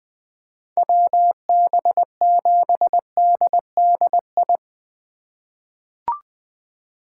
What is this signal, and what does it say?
Morse code, heard as a keyed tone, about 20 wpm, 700 Hz. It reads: WB7DDI
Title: USA Call Signs - 3x Character Spacing - 20wpm